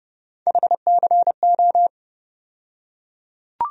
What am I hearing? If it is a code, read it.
Morse code, heard as a keyed tone, about 30 wpm, 700 Hz. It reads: HCO
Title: USA Call Sign Postfixes - 3 Letters 30wpm